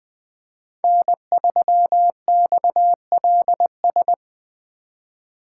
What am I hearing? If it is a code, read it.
Morse code, heard as a keyed tone, about 20 wpm, 700 Hz. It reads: N3XLS
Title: 2023 POTA Hunter Call Signs - 20wpm